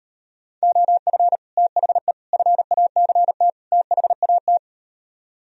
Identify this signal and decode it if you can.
Morse code, heard as a keyed tone, about 38 wpm, 700 Hz. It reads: OF THE FACT THAT